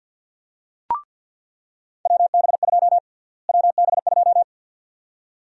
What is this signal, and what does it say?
Morse code, heard as a keyed tone, about 49 wpm, 700 Hz. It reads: WB2 WB2
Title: USA Call Sign Prefixes - Repeated 2x - 50wpm